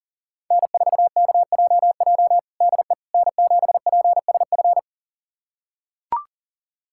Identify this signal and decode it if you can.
Morse code, heard as a keyed tone, about 40 wpm, 700 Hz. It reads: N4KJJ DE N7PSF
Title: USA Callsign DE Callsign - 5 Characters - 40wpm